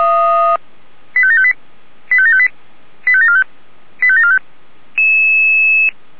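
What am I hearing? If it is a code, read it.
Five-tone selective call (ZVEI), heard as quick runs of five tones, tones of about 70 ms. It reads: 85758 85758 85745 85745
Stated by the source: ZVEI five-tone call 85735 received on an SDR.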